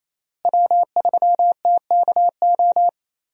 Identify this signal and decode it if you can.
Morse code, heard as a keyed tone, about 27 wpm, 700 Hz. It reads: W3TKO
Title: USA Call Signs - 5 Characters - Repeated 3x - 28wpm